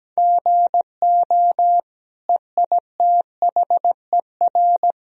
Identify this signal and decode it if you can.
Morse code, heard as a keyed tone, about 17 wpm, 700 Hz. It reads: GO EITHER
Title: Sentences from Top 1000 Words - Review Part 4 - 17wpm